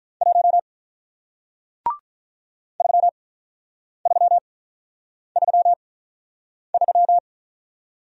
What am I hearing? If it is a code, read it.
Morse code, heard as a keyed tone, about 48 wpm, spaced as for 9 wpm, 700 Hz. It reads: 1 3333